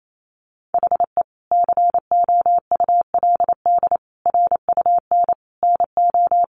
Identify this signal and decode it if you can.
Morse code, heard as a keyed tone, about 28 wpm, 700 Hz. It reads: HE COULD RUN NO
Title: Sentences from Top 300 Words 28wpm